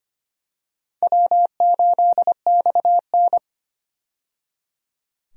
Morse code, heard as a keyed tone, about 25 wpm, 700 Hz. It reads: W8XN
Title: USA Call Signs - 4 Characters - 25wpm